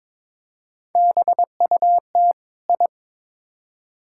Morse code, heard as a keyed tone, about 22 wpm, 700 Hz. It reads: BUT I
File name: practice / 22wpm / Build A Phrase - 2-Word Phrases - 22wpm